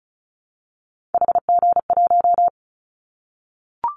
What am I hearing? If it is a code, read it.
Morse code, heard as a keyed tone, about 34 wpm, 700 Hz. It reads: HG1
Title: Callsign Prefixes for Top 25 DXCC - Repeated 3x - 35wpm